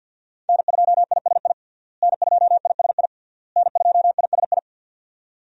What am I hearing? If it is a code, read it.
Morse code, heard as a keyed tone, about 47 wpm, 700 Hz. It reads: N2ISI N2ISI N2ISI
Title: USA Call Signs - 5 Characters - Repeated 3x - 50wpm